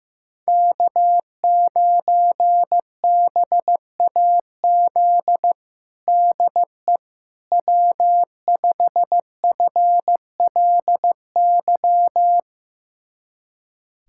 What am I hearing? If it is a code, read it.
Morse code, heard as a keyed tone, about 15 wpm, 700 Hz. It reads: K9BAZ DE W5FLY
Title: USA Callsign DE Callsign - 5 Characters - 15wpm